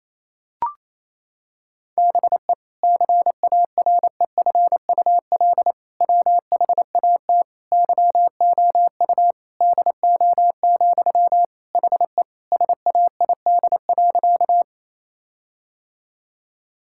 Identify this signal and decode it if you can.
Morse code, heard as a keyed tone, about 28 wpm, 700 Hz. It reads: BE CAREFUL WHAT YOU DO, HE SAID.
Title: Sentences from Top 1600 Words 28wpm